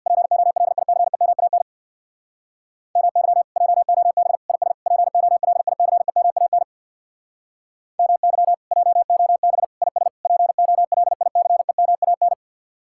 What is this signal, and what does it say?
Morse code, heard as a keyed tone, about 63 wpm, 700 Hz. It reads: POLICEMAN MY JOB IS POLICEMAN MY JOB IS POLICEMAN